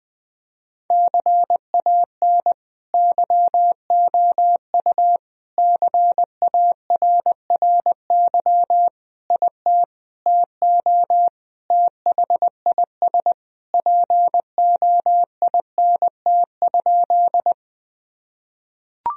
Morse code, heard as a keyed tone, about 21 wpm, 700 Hz. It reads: CAN YOU CARRY IT TO THIS POINT?
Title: Sentences from Top 1000 Words - Review Part 1 - 20wpm